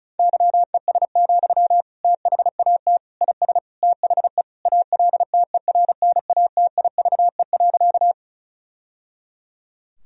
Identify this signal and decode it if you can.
Morse code, heard as a keyed tone, about 34 wpm, 700 Hz. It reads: YES, THAT IS THE ALTERNATIVE.
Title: Sentences from Top 1000 Words - Review Part 4 - 35wpm